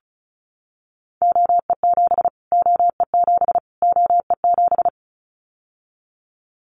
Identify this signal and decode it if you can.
Morse code, heard as a keyed tone, about 34 wpm, 700 Hz. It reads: OE7 OE7 OE7
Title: Callsign Prefixes for Top 25 DXCC - Repeated 3x - 35wpm